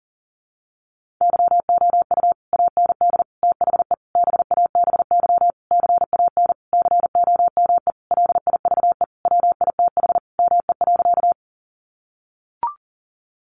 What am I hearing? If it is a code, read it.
Morse code, heard as a keyed tone, about 40 wpm, 700 Hz. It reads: YOU AND THE BABY CAN COME LIVE WITH ME.
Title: Sentences from Top 1300 Words 40wpm